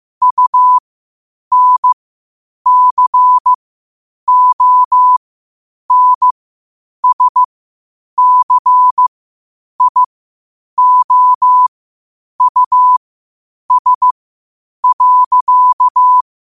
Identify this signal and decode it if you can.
Morse code, heard as a keyed tone, about 15 wpm, spaced as for 8 wpm, 1000 Hz. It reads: UNCONSCIOUS.
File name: qotdfiles / 8wpm